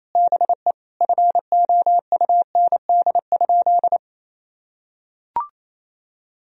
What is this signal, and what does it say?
Morse code, heard as a keyed tone, about 28 wpm, 700 Hz. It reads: BE FOUND?
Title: Sentences from Top 300 Words and Abbreviations 28wpm